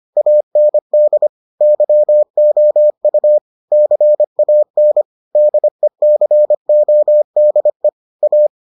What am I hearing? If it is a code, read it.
Morse code, heard as a keyed tone, about 25 wpm, 600 Hz. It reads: AND YOU CAN DECODE A